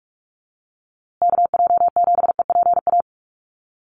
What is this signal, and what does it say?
Morse code, heard as a keyed tone, about 44 wpm, 700 Hz. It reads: KJ7EPA
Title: USA Call Signs Repeated 2x 45wpm